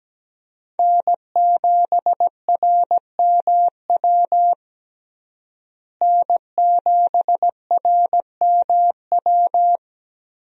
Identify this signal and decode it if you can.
Morse code, heard as a keyed tone, about 17 wpm, 700 Hz. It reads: N7RMW N7RMW
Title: USA Call Signs - 5 Characters - Repeated 3x - 17wpm